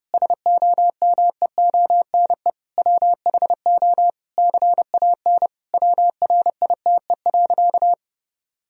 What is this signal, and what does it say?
Morse code, heard as a keyed tone, about 30 wpm, 700 Hz. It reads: SOMEONE WHO CAN WRITE.
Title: Sentences from Top 1000 Words - Review Part 1 - 30wpm